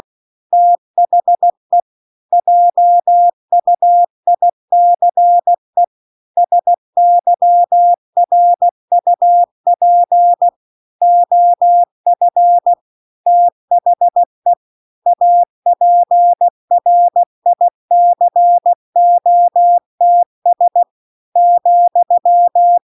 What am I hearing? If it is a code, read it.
Morse code, heard as a keyed tone, about 16 wpm, 700 Hz. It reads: THE JUICE SYRUP OF THE APRICOTS ,